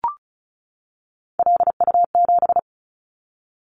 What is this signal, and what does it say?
Morse code, heard as a keyed tone, about 36 wpm, 700 Hz. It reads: LU7
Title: Callsign Prefixes for Top 50 DXCC - 35wpm